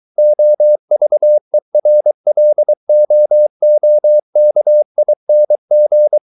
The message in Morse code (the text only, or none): OVERLOOKING